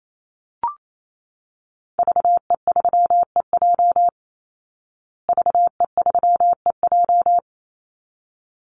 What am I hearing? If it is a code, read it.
Morse code, heard as a keyed tone, about 28 wpm, 700 Hz. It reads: VE3EJ VE3EJ
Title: Call Signs Repeated 3x - 28wpm